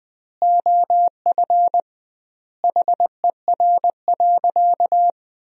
Morse code, heard as a keyed tone, about 20 wpm, 700 Hz. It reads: OF HER.
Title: 2x Word-Spacing - Sentences from Top 200 Words - 20wpm